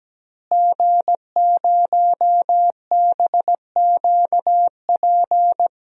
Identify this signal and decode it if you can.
Morse code, heard as a keyed tone, about 17 wpm, 700 Hz. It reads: G0BQP
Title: DXCC-100 Call Signs - 5 Characters - Repeated 3x - 17wpm